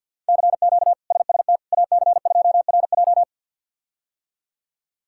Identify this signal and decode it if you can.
Morse code, heard as a keyed tone, about 52 wpm, 700 Hz. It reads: CQ SST AC2RJ
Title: K1USN SST QSO Practice - 50wpm